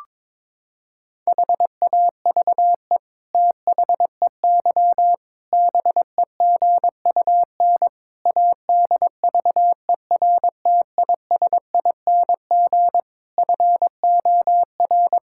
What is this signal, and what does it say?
Morse code, heard as a keyed tone, about 22 wpm, 700 Hz. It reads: HAVE THEY BEGUN ADVERTISING FOR
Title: Sentences from Top 1000 Words - Review Part 6 - 22wpm